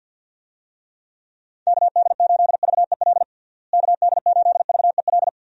Morse code, heard as a keyed tone, about 52 wpm, 700 Hz. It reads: KD8VEL KD8VEL
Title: USA Call Signs - 6 Characters - Repeated 3x - 50wpm